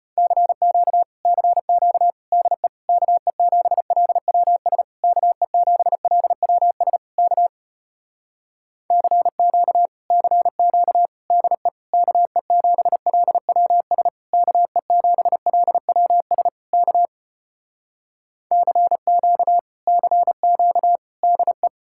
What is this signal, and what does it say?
Morse code, heard as a keyed tone, about 33 wpm, 700 Hz. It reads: CQ CQ DE KE7LWS KE7LWS K CQ CQ DE KE7LWS KE7LWS K CQ CQ DE